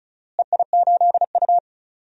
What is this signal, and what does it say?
Morse code, heard as a keyed tone, about 36 wpm, 700 Hz. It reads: EI8U
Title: DXCC-100 Call Signs - 4 Characters - Repeated 2x - 35wpm